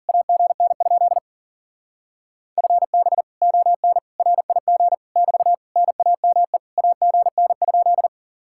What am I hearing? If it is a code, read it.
Morse code, heard as a keyed tone, about 52 wpm, 700 Hz. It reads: AGN? FB ON RIG <BT> NAME AGN?